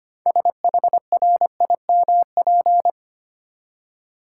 Morse code, heard as a keyed tone, about 25 wpm, 690 Hz. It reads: SHRIMP
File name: practice / 25wpm / Easy Words with Hints - 25wpm